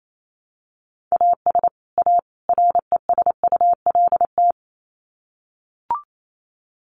Morse code, heard as a keyed tone, about 28 wpm, 700 Hz. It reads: AS A RESULT
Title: Sets of 3 Words 28wpm